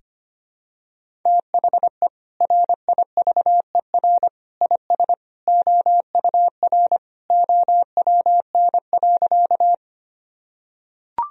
Morse code, heard as a keyed tone, about 25 wpm, 700 Hz. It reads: THE RIVER IS OUR OWN.
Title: Sentences from Top 1000 Words - Review Part 1 - 25wpm